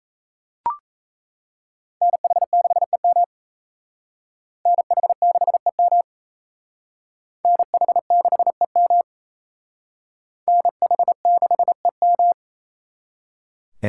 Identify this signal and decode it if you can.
Morse code, heard as a keyed tone, about 35 wpm, 700 Hz. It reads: NH6EM NH6EM NH6EM NH6EM
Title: Speed Racing - USA Call Signs 28wpm